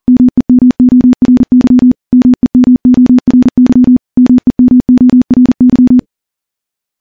Morse code, heard as a keyed tone, about 40 wpm, 250 Hz. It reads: MEMORY MEMORY MEMORY
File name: morse / morse-training-40WPM